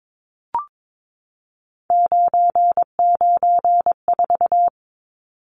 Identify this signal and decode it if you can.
Morse code, heard as a keyed tone, about 22 wpm, 700 Hz. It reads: 994
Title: Numbers - 3-Digit 22wpm